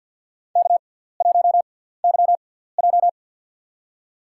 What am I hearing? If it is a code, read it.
Morse code, heard as a keyed tone, about 50 wpm, spaced as for 17 wpm, 700 Hz. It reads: K1YJ
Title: USA Call Signs - 4 Characters - 6x Character Spacing - 50wpm